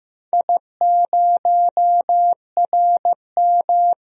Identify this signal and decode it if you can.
Morse code, heard as a keyed tone, about 15 wpm, 700 Hz. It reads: I0RM
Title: DXCC-100 Call Signs - 4 Characters - Repeated 2x - 15wpm